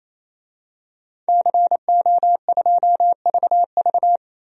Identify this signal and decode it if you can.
Morse code, heard as a keyed tone, about 28 wpm, 700 Hz. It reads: CO2VV
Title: DXCC-100 Call Signs - 5 Characters - 28wpm